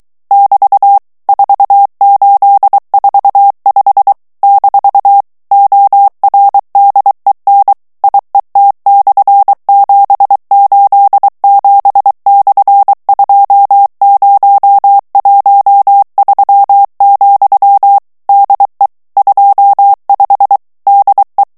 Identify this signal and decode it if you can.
Morse code, heard as a keyed tone, about 23 wpm, 800 Hz. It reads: <BT> 4845 - ORDEN IET/787/2013, DE 25 DE
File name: ESPAGNOL TEXTE EN CLAIR 2 VITESSE 22 MOTS MINUTES